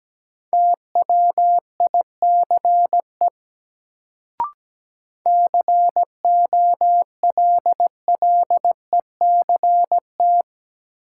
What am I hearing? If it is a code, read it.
Morse code, heard as a keyed tone, about 17 wpm, 700 Hz. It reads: TWICE COLLECT
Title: Sentences from Top 1600 Words - New Word Drill Alt - 17wpm